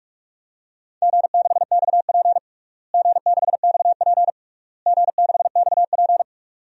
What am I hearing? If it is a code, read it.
Morse code, heard as a keyed tone, about 46 wpm, 700 Hz. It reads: G6XP G6XP G6XP
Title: DXCC-100 Call Signs - 4 Characters - Repeated 3x - 45wpm